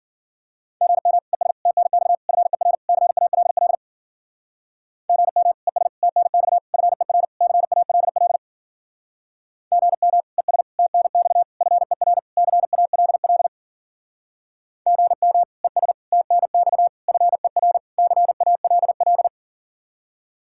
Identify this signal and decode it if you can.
Morse code, heard as a keyed tone, about 52 wpm, 700 Hz. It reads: GM ES TNX FER CALL GM ES TNX FER CALL GM ES TNX FER CALL GM ES TNX FER CALL